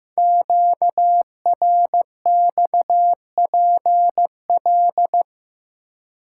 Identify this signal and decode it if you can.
Morse code, heard as a keyed tone, about 15 wpm, 700 Hz. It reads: QRXPL
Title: Code Groups - 5 Letters - 15wpm